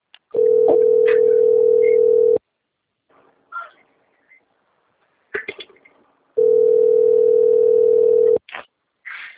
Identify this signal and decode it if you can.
Ringback tone (North America), 440 + 480 Hz.
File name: rg-1600-unknown- - - .36502